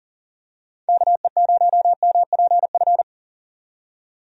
Morse code, heard as a keyed tone, about 39 wpm, 700 Hz. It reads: KE0MPF
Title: USA Call Signs - 6 Characters - 40wpm